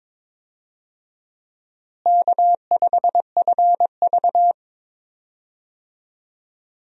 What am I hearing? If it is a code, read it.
Morse code, heard as a keyed tone, about 22 wpm, 700 Hz. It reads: K5FV
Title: USA Call Signs - 4 Characters - Repeated 3x - 22wpm